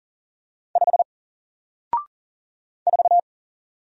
Morse code, heard as a keyed tone, about 40 wpm, 700 Hz. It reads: 5 4
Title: Single Character (Letter:Number:Punctuation:Prosign) 40wpm